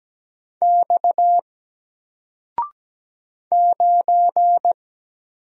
Morse code, heard as a keyed tone, about 17 wpm, 700 Hz. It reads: X 9
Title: Single Letter-Number 17wpm